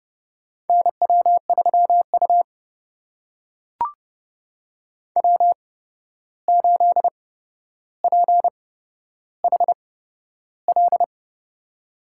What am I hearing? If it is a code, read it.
Morse code, heard as a keyed tone, about 29 wpm, spaced as for 12 wpm, 700 Hz. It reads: NW3U W8PHL
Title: USA Call Signs - 8x Character Spacing - 30wpm